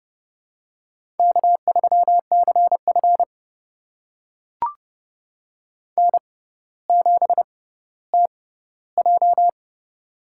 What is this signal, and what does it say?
Morse code, heard as a keyed tone, about 30 wpm, spaced as for 15 wpm, 700 Hz. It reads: K3CF N7TJ
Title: USA Call Signs - 6x Character Spacing - 30wpm